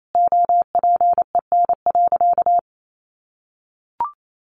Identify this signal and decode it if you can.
Morse code, heard as a keyed tone, about 28 wpm, 700 Hz. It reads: OPEN.
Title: Sentences from Top 1300 Words 28wpm